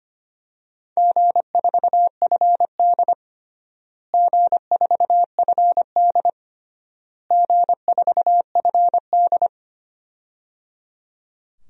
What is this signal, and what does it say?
Morse code, heard as a keyed tone, about 25 wpm, 700 Hz. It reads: G4FD G4FD G4FD
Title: DXCC-100 Call Signs - 4 Characters - Repeated 3x - 25wpm